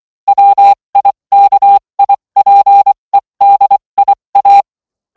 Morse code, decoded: WIKIPEDIA